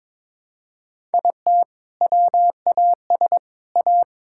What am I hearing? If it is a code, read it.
Morse code, heard as a keyed tone, about 22 wpm, 700 Hz. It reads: IT WAS A
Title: Sentences from Top 1000 Words - Review Part 5 - 22wpm